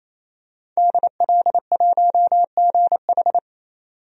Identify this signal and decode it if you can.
Morse code, heard as a keyed tone, about 28 wpm, 700 Hz. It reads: DL1GH
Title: DXCC-100 Call Signs - 5 Characters - 28wpm